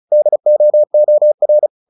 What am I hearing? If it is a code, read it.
Morse code, heard as a keyed tone, about 35 wpm, 600 Hz. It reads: DOOR
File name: the old man and the sea / the old man and the sea 1 35wpm